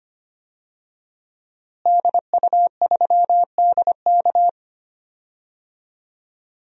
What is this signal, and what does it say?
Morse code, heard as a keyed tone, about 25 wpm, 700 Hz. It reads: DU3DK
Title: DXCC-100 Call Signs - 5 Characters - Repeated 3x - 25wpm